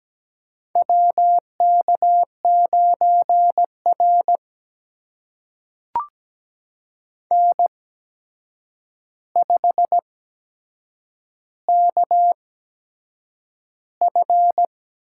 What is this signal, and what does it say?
Morse code, heard as a keyed tone, about 17 wpm, spaced as for 7 wpm, 700 Hz. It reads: WK9R N5KF
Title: USA Call Signs - 4 Characters - 8x Character Spacing - 17wpm